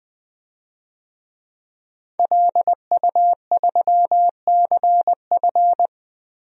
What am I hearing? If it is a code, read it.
Morse code, heard as a keyed tone, about 19 wpm, 700 Hz. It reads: LU3CF
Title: DXCC-100 Call Signs - 5 Characters - Repeated 3x - 20wpm